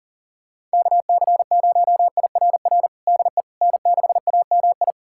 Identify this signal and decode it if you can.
Morse code, heard as a keyed tone, about 40 wpm, 700 Hz. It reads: KC0IRR DE N6AMI
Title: USA Callsign DE Callsign 40wpm